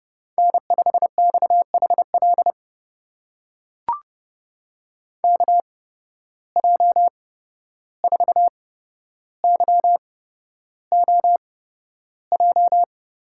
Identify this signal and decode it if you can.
Morse code, heard as a keyed tone, about 30 wpm, spaced as for 12 wpm, 700 Hz. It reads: N5XHL KJ4YOJ
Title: USA Call Signs - 8x Character Spacing - 30wpm